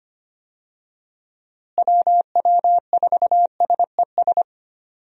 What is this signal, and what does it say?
Morse code, heard as a keyed tone, about 25 wpm, 700 Hz. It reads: WW4SES
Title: USA Call Signs - 6 Characters - Repeated 2x - 25wpm